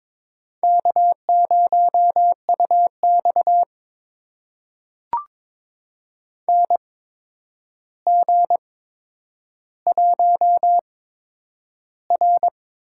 Morse code, heard as a keyed tone, about 22 wpm, spaced as for 9 wpm, 700 Hz. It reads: K0UX NG1R
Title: USA Call Signs - 4 Characters - 8x Character Spacing - 22wpm